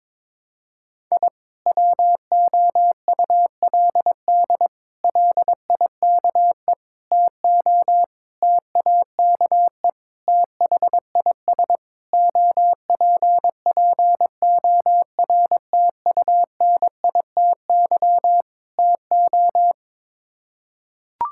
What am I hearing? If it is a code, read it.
Morse code, heard as a keyed tone, about 22 wpm, 700 Hz. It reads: I WOULD LIKE TO TAKE THIS OPPORTUNITY TO
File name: practice / 22wpm / Sets of 8 Words - 22wpm